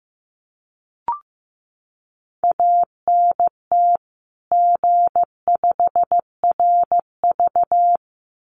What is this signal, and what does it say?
Morse code, heard as a keyed tone, about 15 wpm, 700 Hz. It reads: ANT G5RV
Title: Common QSO Phrases 15wpm